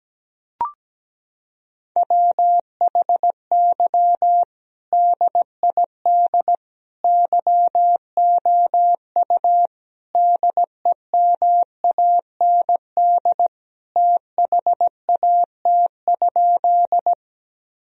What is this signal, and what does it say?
Morse code, heard as a keyed tone, about 17 wpm, 700 Hz. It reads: WHY DID YOU DEMAND THAT?
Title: Sentences from Top 1000 Words - Review Part 4 - 17wpm